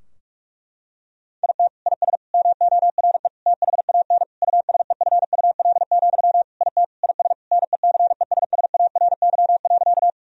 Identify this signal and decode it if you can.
Morse code, heard as a keyed tone, about 46 wpm, 700 Hz. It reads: IT IS MORE THAN USEFUL, IT IS NECESSARY.
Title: Sentences from Top 1000 Words - Review Part 4 - 45wpm